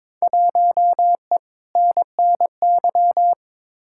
Morse code, heard as a keyed tone, about 22 wpm, 700 Hz. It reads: 1E NNY
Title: Field Day Exchanges - 22wpm